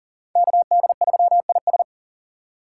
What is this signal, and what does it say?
Morse code, heard as a keyed tone, about 51 wpm, 700 Hz. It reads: KD3IS